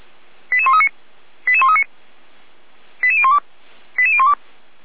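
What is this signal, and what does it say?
Five-tone selective call (ZVEI), heard as quick runs of five tones, tones of about 70 ms. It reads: 88139 88139 88012 88012
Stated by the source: ZVEI five-tone call received by SDR, from the feuerwehr-orlamuende.org fire brigade audio server.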